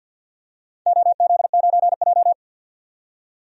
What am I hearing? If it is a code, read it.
Morse code, heard as a keyed tone, about 46 wpm, 700 Hz. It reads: OZ9J
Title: DXCC-100 Call Signs - 4 Characters - 50wpm